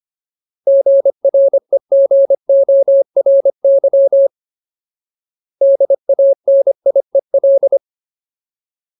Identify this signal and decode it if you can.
Morse code, heard as a keyed tone, about 25 wpm, 550 Hz. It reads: GREGORY DANIEL
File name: Ultimate Word List / 25WPM Ultimate